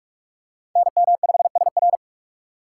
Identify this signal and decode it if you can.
Morse code, heard as a keyed tone, about 42 wpm, 700 Hz. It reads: NM5SR